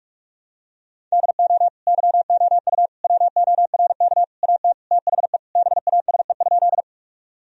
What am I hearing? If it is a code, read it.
Morse code, heard as a keyed tone, about 46 wpm, 700 Hz. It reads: DO YOU WORK AT THE BASE?
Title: Sentences from Top 1000 Words - Review Part 1 - 45wpm